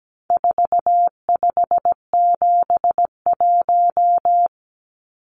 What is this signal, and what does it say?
Morse code, heard as a keyed tone, about 17 wpm, 700 Hz. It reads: 4571
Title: Numbers - 4-Digit 17wpm